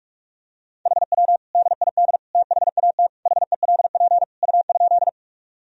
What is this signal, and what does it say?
Morse code, heard as a keyed tone, about 46 wpm, 700 Hz. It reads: HW DID THAT HELP U?
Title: Sentences from Top 200 Words and Abbreviations 45wpm